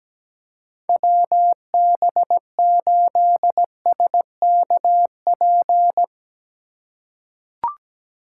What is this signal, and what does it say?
Morse code, heard as a keyed tone, about 17 wpm, 700 Hz. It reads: WB8SKP
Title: K1USN SST QSO Practice - 17wpm